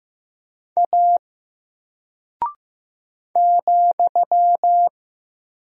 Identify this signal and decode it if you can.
Morse code, heard as a keyed tone, about 15 wpm, 700 Hz. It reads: A ,
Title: Single Character (Letter:Number:Punctuation:Prosign) 15wpm